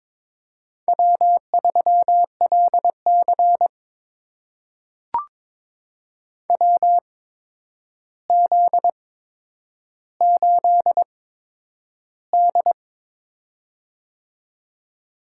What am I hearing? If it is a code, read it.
Morse code, heard as a keyed tone, about 21 wpm, spaced as for 9 wpm, 700 Hz. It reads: W3LC WZ8D
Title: USA Call Signs - 4 Characters - 8x Character Spacing - 22wpm